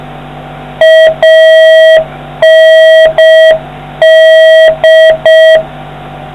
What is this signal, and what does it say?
Morse code, heard as a keyed tone, about 6 wpm, 640 Hz, with noise. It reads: AND